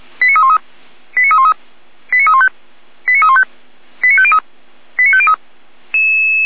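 Five-tone selective call (ZVEI), heard as quick runs of five tones, tones of about 70 ms. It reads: 89313 89313 89315 89315 89503 89503
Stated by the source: Received on an SDR; ZVEI five-tone call.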